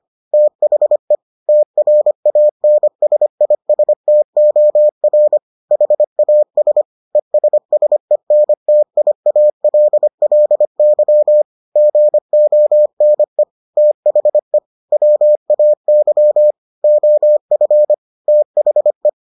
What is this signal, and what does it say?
Morse code, heard as a keyed tone, about 25 wpm, 600 Hz. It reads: THE TRANSISTOR HAS ESSENTIALLY GONE THE WAY OF THE